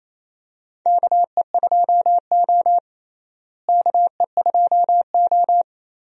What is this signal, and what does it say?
Morse code, heard as a keyed tone, about 28 wpm, 700 Hz. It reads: KE2O KE2O
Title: USA Call Signs - 4 Characters - Repeated 2x - 28wpm